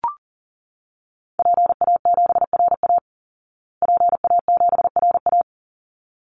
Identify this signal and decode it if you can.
Morse code, heard as a keyed tone, about 37 wpm, 700 Hz. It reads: PA7RA PA7RA
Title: Call Signs Repeated 3x - 40wpm